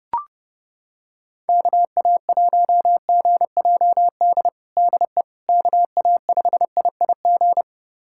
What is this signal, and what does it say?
Morse code, heard as a keyed tone, about 30 wpm, 700 Hz. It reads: KA1GJD DE KA5IIG
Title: USA Callsign DE Callsign - 6 Characters - 30wpm